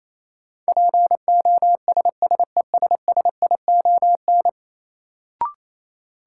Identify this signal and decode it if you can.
Morse code, heard as a keyed tone, about 28 wpm, 700 Hz. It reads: POSSESSION
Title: Sentences from Top 1400 Words - New Word Drill - 28wpm